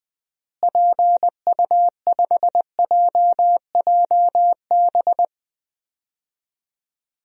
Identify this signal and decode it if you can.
Morse code, heard as a keyed tone, about 20 wpm, 700 Hz. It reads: PU5JJB
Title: DXCC-100 Call Signs - 6 Characters - Repeated 3x - 20wpm